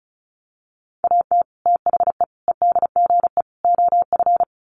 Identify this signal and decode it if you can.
Morse code, heard as a keyed tone, about 34 wpm, 700 Hz. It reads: AT THE EDGE OF
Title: Sets of 4 Words 35wpm